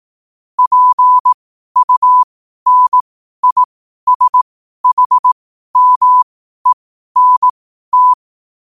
Morse code, heard as a keyed tone, about 18 wpm, spaced as for 12 wpm, 1000 Hz. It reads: PUNISHMENT
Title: 12 WPM morse code quotes for Thu, 29 Feb 2024